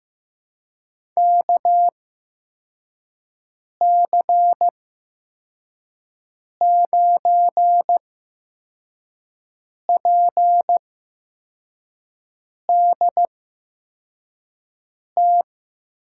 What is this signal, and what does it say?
Morse code, heard as a keyed tone, about 15 wpm, spaced as for 4 wpm, 700 Hz. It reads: KC9PDT
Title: USA Call Signs - 6 Characters - 8x Character Spacing - 15wpm